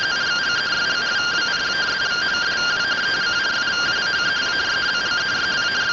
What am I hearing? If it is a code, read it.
Two-tone teleprinter signal, about 170 Hz shift.